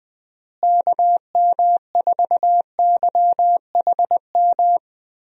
Morse code, heard as a keyed tone, about 20 wpm, 700 Hz. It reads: KM4YHM
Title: USA Call Signs - 6 Characters - Repeated 3x - 20wpm